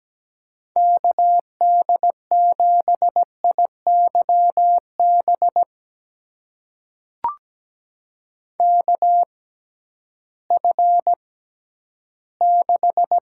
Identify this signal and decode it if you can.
Morse code, heard as a keyed tone, about 17 wpm, spaced as for 11 wpm, 700 Hz. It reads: KD7IYB KF6
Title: USA Call Signs - 6 Characters - 6x Character Spacing - 17wpm